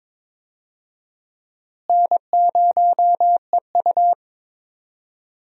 Morse code, heard as a keyed tone, about 22 wpm, 700 Hz. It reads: N0EU
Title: USA Call Signs - 4 Characters - Repeated 3x - 22wpm